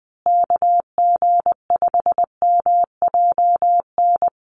Morse code, heard as a keyed tone, about 20 wpm, 700 Hz. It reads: KG5MJN
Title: USA Call Signs Repeated 2x 20wpm